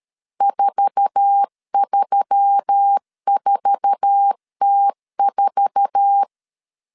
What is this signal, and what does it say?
Morse code, heard as a keyed tone, about 13 wpm, 800 Hz. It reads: 434T4
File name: LEZIONI / 8 YJ45 / 8 YJ4 12wpm 2151